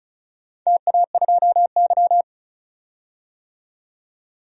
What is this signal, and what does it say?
Morse code, heard as a keyed tone, about 34 wpm, 700 Hz. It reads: TA2Y
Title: DXCC-100 Call Signs - 4 Characters - 35wpm